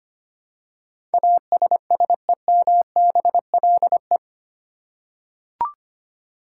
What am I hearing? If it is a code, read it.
Morse code, heard as a keyed tone, about 25 wpm, 700 Hz. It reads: ASSEMBLE
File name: 8 Letter Words - First 3 Letters Given - 25wpm